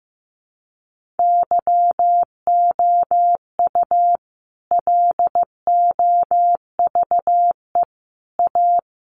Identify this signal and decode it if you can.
Morse code, heard as a keyed tone, about 15 wpm, 700 Hz. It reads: YOU LOVE A
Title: Sentences from Top 500 Words 15wpm